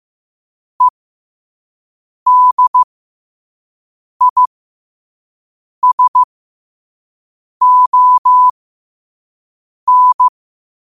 Morse code, heard as a keyed tone, about 15 wpm, spaced as for 5 wpm, 1000 Hz. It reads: EDISON
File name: qotdfiles / 5wpm